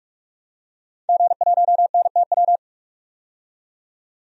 Morse code, heard as a keyed tone, about 46 wpm, 700 Hz. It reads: G1NTW
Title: DXCC-100 Call Signs - 5 Characters - 45wpm